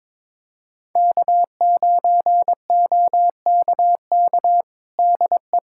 Morse code, heard as a keyed tone, about 21 wpm, 700 Hz. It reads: K9OKK DE
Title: USA Callsign DE Callsign - 5 Characters - 22wpm